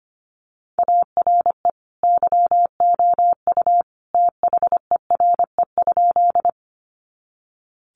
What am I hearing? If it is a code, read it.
Morse code, heard as a keyed tone, about 25 wpm, 700 Hz. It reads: ARE YOU THERE?
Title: Sentences from Top 100 Words 25wpm